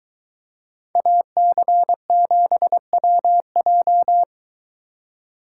Morse code, heard as a keyed tone, about 23 wpm, 700 Hz. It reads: AC7WJ